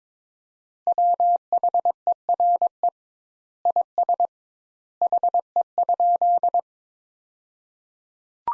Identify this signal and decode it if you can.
Morse code, heard as a keyed tone, about 22 wpm, 700 Hz. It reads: WHERE IS HE?
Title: Sentences Top 200 Words - 22wpm